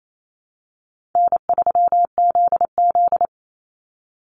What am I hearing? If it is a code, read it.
Morse code, heard as a keyed tone, about 28 wpm, 700 Hz. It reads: N3ZZ
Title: Call Signs Repeated 2x - 28wpm